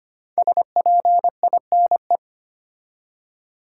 Morse code, heard as a keyed tone, about 25 wpm, 700 Hz. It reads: SPINE
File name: 5 Letter Words - First 3 Letters Given - 25wpm